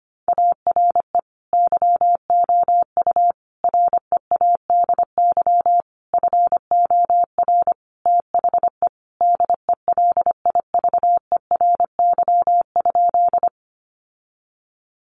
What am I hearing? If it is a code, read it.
Morse code, heard as a keyed tone, about 25 wpm, 700 Hz. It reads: ARE YOU READY FOR THE DELIVERY?
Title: Sentences from Top 1500 Words 25wpm